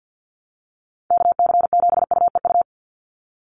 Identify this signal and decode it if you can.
Morse code, heard as a keyed tone, about 49 wpm, 700 Hz. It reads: KC7UEU
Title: USA Call Signs 50wpm